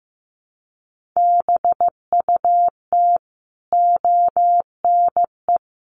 Morse code, heard as a keyed tone, about 15 wpm, 700 Hz. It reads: BUT ONE
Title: Sentences from Top 1400 Words 15wpm